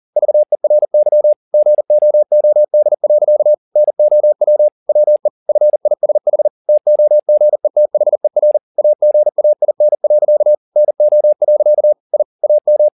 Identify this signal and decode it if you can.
Morse code, heard as a keyed tone, about 40 wpm, 600 Hz. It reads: VERY GOOD. NOW WE FISH TOGETHER AGAIN. NO. I AM
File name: the old man and the sea 2 40wpm